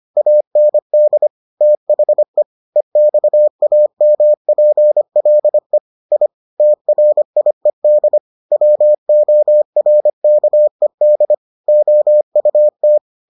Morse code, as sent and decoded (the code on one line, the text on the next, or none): .- -. -.. / - .... / . -..- .- -- .--. .-.. / .. / - .-. .. -.. / .-- --- .-. -.- . -.. / --- ..- -
AND THE EXAMPLE I TRIED WORKED OUT